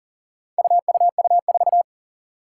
Morse code, heard as a keyed tone, about 40 wpm, 700 Hz. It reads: UUU4
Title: Four Challenging Letter-Number Combinations 40wpm